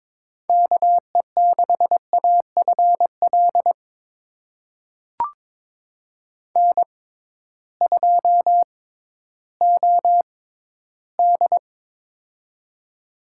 Morse code, heard as a keyed tone, about 22 wpm, spaced as for 13 wpm, 700 Hz. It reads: KE6AFL N2OD
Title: USA Call Signs - 6x Character Spacing - 22wpm